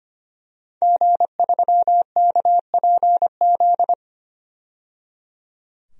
Morse code, heard as a keyed tone, about 25 wpm, 700 Hz. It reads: G3KPZ
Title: DXCC-100 Call Signs - 5 Characters - 25wpm